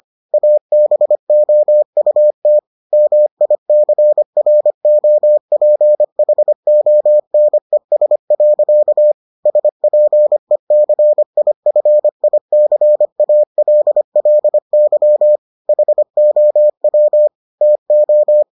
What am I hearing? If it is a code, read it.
Morse code, heard as a keyed tone, about 25 wpm, 600 Hz. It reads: ABOUT MICROPHONES. SPECIFICALLY HOW TO